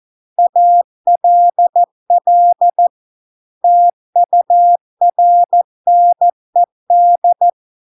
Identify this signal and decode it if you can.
Morse code, heard as a keyed tone, about 14 wpm, 700 Hz. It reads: ALL TURNED